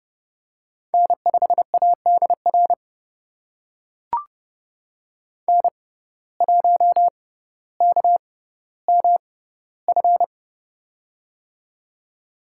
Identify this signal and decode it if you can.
Morse code, heard as a keyed tone, about 32 wpm, spaced as for 16 wpm, 700 Hz. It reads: N5ADR N1KMF
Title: USA Call Signs - 5 Characters - 6x Character Spacing - 30wpm